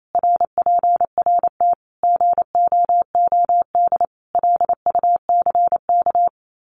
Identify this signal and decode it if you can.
Morse code, heard as a keyed tone, about 28 wpm, 700 Hz. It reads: RPRT GOOD LUCK
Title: Common QSO Phrases 28wpm